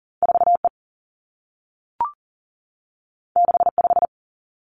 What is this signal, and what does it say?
Morse code, heard as a keyed tone, about 42 wpm, 700 Hz. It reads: 4 E 6 5
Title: Two Challenging Letter-Number Combinations 40wpm